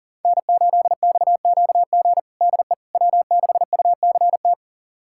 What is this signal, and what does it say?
Morse code, heard as a keyed tone, about 40 wpm, 700 Hz. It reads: N8XQG DE W6UCT
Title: USA Callsign DE Callsign - 5 Characters - 40wpm